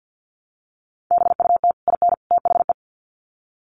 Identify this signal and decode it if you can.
Morse code, heard as a keyed tone, about 50 wpm, 700 Hz. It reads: BUT IN THE
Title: Sets of 3 Words 50wpm